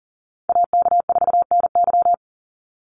Morse code, heard as a keyed tone, about 39 wpm, 700 Hz. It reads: AK4NY
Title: Call Signs - 40wpm